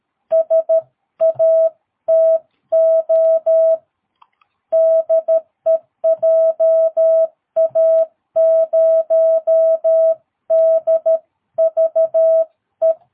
Morse code, heard as a keyed tone, about 13 wpm, 650 Hz. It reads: SATO DE JA0DVE
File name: jh8cbh / 2022cw